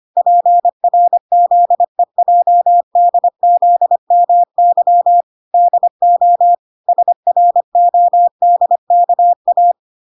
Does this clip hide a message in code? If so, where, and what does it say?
Yes: Morse code from 0.2 to 9.7 s, heard as a keyed tone, about 25 wpm, 700 Hz. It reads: PRZEJDZMY DO SRODKA